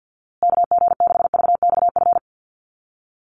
Morse code, heard as a keyed tone, about 50 wpm, 700 Hz. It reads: KG6VXF
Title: USA Call Signs Repeated 2x 50wpm